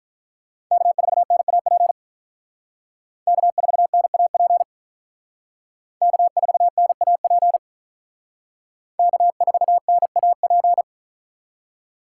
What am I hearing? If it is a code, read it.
Morse code, heard as a keyed tone, about 45 wpm, 700 Hz. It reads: K4NAP K4NAP K4NAP K4NAP